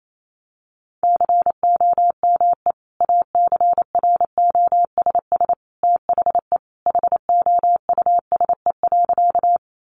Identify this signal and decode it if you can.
Morse code, heard as a keyed tone, about 28 wpm, 700 Hz. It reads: COME ACROSS THE HOUSE.
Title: Sentences from Top 700 Words 28wpm